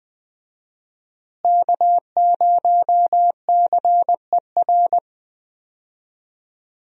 Morse code, heard as a keyed tone, about 20 wpm, 700 Hz. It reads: K0CER
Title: USA Call Signs - 5 Characters - Repeated 2x - 20wpm